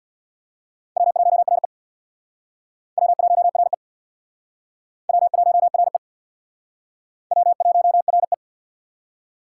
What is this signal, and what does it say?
Morse code, heard as a keyed tone, about 63 wpm, 700 Hz. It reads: W1RE W1RE W1RE W1RE